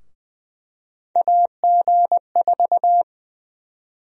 Morse code, heard as a keyed tone, about 20 wpm, 700 Hz. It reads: AG4
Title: USA Call Sign Prefixes - 20wpm